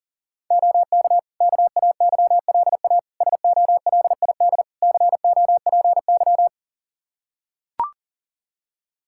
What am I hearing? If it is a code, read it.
Morse code, heard as a keyed tone, about 40 wpm, 700 Hz. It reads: OK KAYLA SOLID COPY
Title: K1USN SST QSO Practice 40wpm